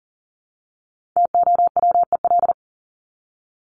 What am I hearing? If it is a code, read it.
Morse code, heard as a keyed tone, about 40 wpm, 700 Hz. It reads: TOWEL
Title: Top 1501-1750 Words 40wpm